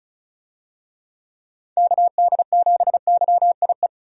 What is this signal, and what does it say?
Morse code, heard as a keyed tone, about 36 wpm, 700 Hz. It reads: KD7YIE
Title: USA Call Signs - 6 Characters - 35wpm